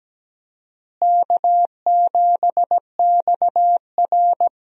Morse code, heard as a keyed tone, about 17 wpm, 700 Hz. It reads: K7XR
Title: USA Call Signs - 4 Characters - Repeated 2x - 17wpm